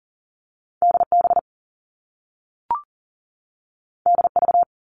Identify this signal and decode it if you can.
Morse code, heard as a keyed tone, about 40 wpm, 700 Hz. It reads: DB DV
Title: All Cut Numbers - 2-Digit 40wpm